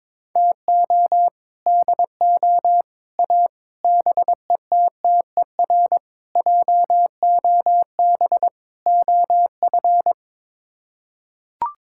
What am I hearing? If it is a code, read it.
Morse code, heard as a keyed tone, about 22 wpm, 700 Hz. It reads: TO DO A BETTER JOB OF
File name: practice / 22wpm / Sets of 6 Words - 22wpm